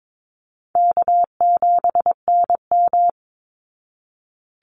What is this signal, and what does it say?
Morse code, heard as a keyed tone, about 22 wpm, 700 Hz. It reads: K7NM
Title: Call Signs Repeated 3x - 22wpm